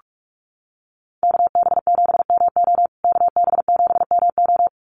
Morse code, heard as a keyed tone, about 44 wpm, 700 Hz. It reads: KB7MO KB7MO
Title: USA Call Signs Repeated 2x 45wpm